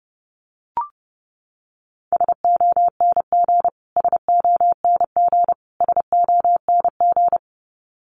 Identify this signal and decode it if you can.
Morse code, heard as a keyed tone, about 30 wpm, 700 Hz. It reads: SONG SONG SONG
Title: Top 500 Words Repeated 30wpm